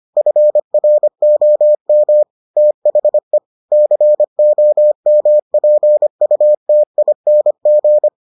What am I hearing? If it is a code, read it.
Morse code, heard as a keyed tone, about 25 wpm, 600 Hz. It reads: FROM THE COMPUTING